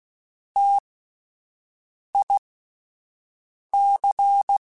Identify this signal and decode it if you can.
Morse code, heard as a keyed tone, about 16 wpm, spaced as for 5 wpm, 800 Hz. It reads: TIC
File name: Alphabetical / Alphabetical mp3s / Ts @5